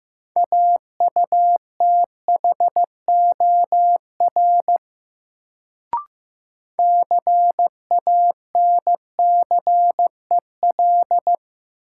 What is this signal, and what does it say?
Morse code, heard as a keyed tone, about 15 wpm, 700 Hz. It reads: AUTHOR CANCEL
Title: Sentences from Top 1200 Words - New Word Drill Alt - 15wpm